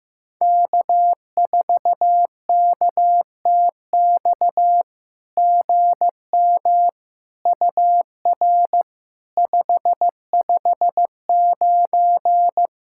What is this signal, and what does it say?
Morse code, heard as a keyed tone, about 15 wpm, 700 Hz. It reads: K4KTX GM UR 559
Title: SOTA QSO Practice 15wpm